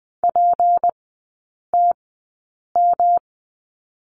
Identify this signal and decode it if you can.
Morse code, heard as a keyed tone, about 20 wpm, spaced as for 8 wpm, 700 Hz. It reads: PTM
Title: 152 Lesson 25e - Letter p - Three Characters 20wpm